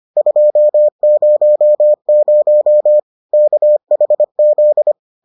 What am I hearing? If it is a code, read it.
Morse code, heard as a keyed tone, about 25 wpm, 600 Hz. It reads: 200 KHZ